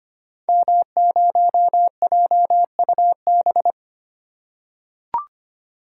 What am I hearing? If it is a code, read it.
Morse code, heard as a keyed tone, about 25 wpm, 700 Hz. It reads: M0JUB
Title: Code Groups - 5 Letter:Numbers - 25wpm